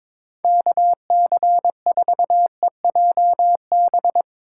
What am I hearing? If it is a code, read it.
Morse code, heard as a keyed tone, about 23 wpm, 700 Hz. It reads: KC4EJB
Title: USA Call Signs - 6 Characters - 4x Character Spacing - 22wpm